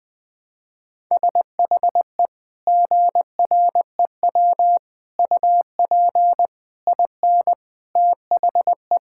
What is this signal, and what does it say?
Morse code, heard as a keyed tone, about 20 wpm, 700 Hz. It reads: SHE GREW UP IN THE
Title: Sentences with 2-Word Phrases - Spoken Beginning - 20wpm